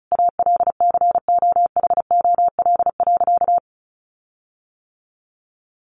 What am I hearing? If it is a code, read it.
Morse code, heard as a keyed tone, about 34 wpm, 700 Hz. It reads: ALCOHOL.
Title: Sentences from Top 1400 Words 35wpm